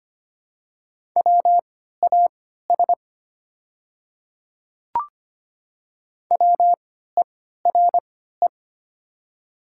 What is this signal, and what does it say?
Morse code, heard as a keyed tone, about 25 wpm, spaced as for 14 wpm, 700 Hz. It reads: WAS WERE
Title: 3x Character Spacing - Top 100 Words - 25wpm